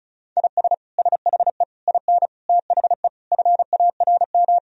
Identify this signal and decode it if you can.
Morse code, heard as a keyed tone, about 36 wpm, 700 Hz. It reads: IS SHE IN THE FARM
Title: Sentences from Top 1000 Words - Review Part 2 - 35wpm